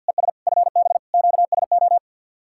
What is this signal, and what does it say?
Morse code, heard as a keyed tone, about 67 wpm, spaced as for 58 wpm, 700 Hz. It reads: ES FB QSO
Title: Speed-Racing Common QSO Phrases - 50wpm